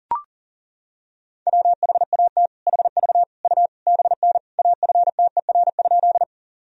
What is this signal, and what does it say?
Morse code, heard as a keyed tone, about 40 wpm, 700 Hz. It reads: WHAT HV U BN AFTER?
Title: Sentences from Top 200 Words and Abbreviations 40wpm